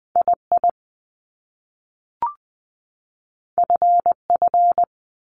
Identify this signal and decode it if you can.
Morse code, heard as a keyed tone, about 20 wpm, 700 Hz. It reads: I I F F
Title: Top Two Letter Combinations in Words 20wpm